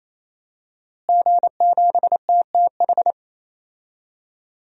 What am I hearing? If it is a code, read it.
Morse code, heard as a keyed tone, about 28 wpm, 700 Hz. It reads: G7TTH
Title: DXCC-100 Call Signs - 5 Characters - Repeated 3x - 28wpm